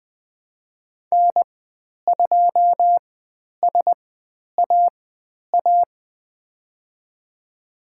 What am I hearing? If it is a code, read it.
Morse code, heard as a keyed tone, about 20 wpm, spaced as for 10 wpm, 700 Hz. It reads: N2SAA
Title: 109 Lesson 18g - Letter m - Call Signs 20wpm